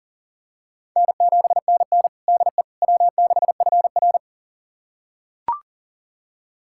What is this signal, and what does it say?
Morse code, heard as a keyed tone, about 40 wpm, 700 Hz. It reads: N7NN DE W6FR
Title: USA Callsign DE Callsign - 4 Characters - 40wpm